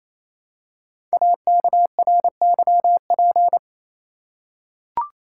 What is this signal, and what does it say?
Morse code, heard as a keyed tone, about 27 wpm, 700 Hz. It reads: AKRYP
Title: Code Groups - 5 Letters - 28wpm